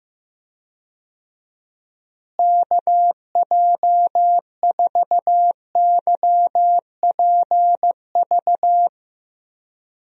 Morse code, heard as a keyed tone, about 15 wpm, 700 Hz. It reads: KJ4YPV
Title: USA Call Signs - 6 Characters - Repeated 3x - 15wpm